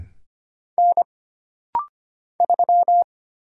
Morse code, heard as a keyed tone, about 25 wpm, 700 Hz. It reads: N 3
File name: Single Character (Letter:Number:Punctuation:Prosign) - Rapid-Fire - 25wpm